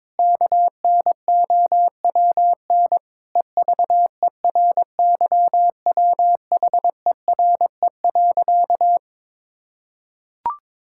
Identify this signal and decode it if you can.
Morse code, heard as a keyed tone, about 22 wpm, 700 Hz. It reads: KNOWN EVERYWHERE.
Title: Sentences from Top 1600 Words 22wpm